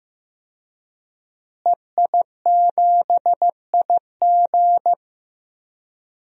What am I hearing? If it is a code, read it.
Morse code, heard as a keyed tone, about 15 wpm, 700 Hz. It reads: EI7IG
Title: DXCC-100 Call Signs - 5 Characters - Repeated 2x - 15wpm